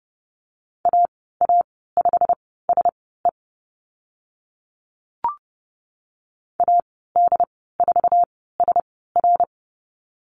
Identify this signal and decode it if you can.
Morse code, heard as a keyed tone, about 30 wpm, spaced as for 17 wpm, 700 Hz. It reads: AA5SE AD4SR
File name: Lesson 15g - Number 5 - Call Signs 30wpm